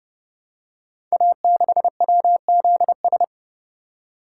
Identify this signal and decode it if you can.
Morse code, heard as a keyed tone, about 30 wpm, 700 Hz. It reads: A6WZS
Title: Code Groups - 5 Letter:Numbers - 30wpm